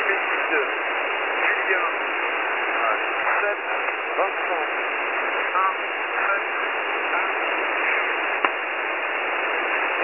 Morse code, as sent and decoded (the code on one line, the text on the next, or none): none